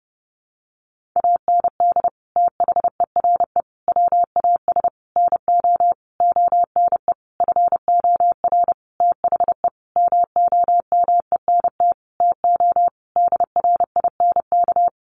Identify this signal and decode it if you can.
Morse code, heard as a keyed tone, about 30 wpm, 700 Hz. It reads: AND THERE WAS NO ONE FOR THE MOMENT TO DRINK
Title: Sentences from Top 1500 Words 30wpm